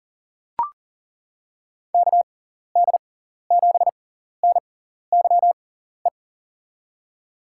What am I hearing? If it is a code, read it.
Morse code, heard as a keyed tone, about 40 wpm, spaced as for 14 wpm, 700 Hz. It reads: KD7NYE
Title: USA Call Signs - 6 Characters - 6x Character Spacing - 40wpm